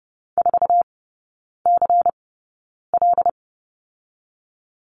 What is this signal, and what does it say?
Morse code, heard as a keyed tone, about 30 wpm, spaced as for 9 wpm, 690 Hz. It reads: 4CL
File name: Lesson 17e - Letter u - Three Characters 30wpm